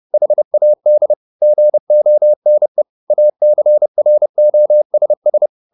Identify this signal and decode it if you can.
Morse code, heard as a keyed tone, about 30 wpm, 600 Hz. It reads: HAD GONE ACROSS